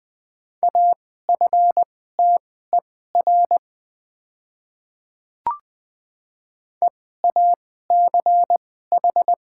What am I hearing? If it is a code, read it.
Morse code, heard as a keyed tone, about 20 wpm, spaced as for 15 wpm, 690 Hz. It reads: AFTER EACH
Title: 2x Character Spacing - Top 200 Words - 20wpm